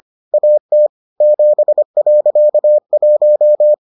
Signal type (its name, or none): Morse code